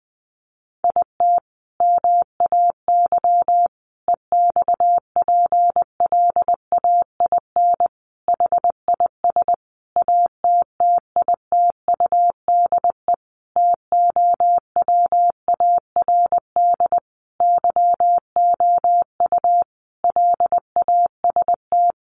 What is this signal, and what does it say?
Morse code, heard as a keyed tone, about 20 wpm, 700 Hz. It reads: IT MAY EXPLAIN HIS ATTITUDE TOWARD YOU LAST
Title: Sentences from Top 1300 Words 20wpm